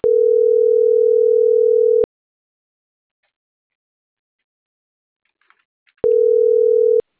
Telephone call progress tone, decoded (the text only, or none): ringback tone